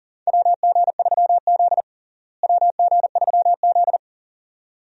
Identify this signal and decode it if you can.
Morse code, heard as a keyed tone, about 37 wpm, 700 Hz. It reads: WG3Z WG3Z
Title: USA Call Signs - 4 Characters - Repeated 2x - 40wpm